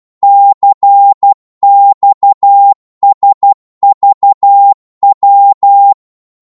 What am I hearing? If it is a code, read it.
Morse code, heard as a keyed tone, about 12 wpm, 800 Hz. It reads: CXSVW
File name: Morse / ltr-12wpm